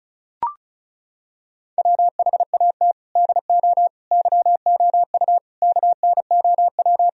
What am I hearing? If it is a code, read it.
Morse code, heard as a keyed tone, about 34 wpm, 700 Hz. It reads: WHAT DO YOU KNOW
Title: Sentences from Top 1000 Words - Review Part 4 - 35wpm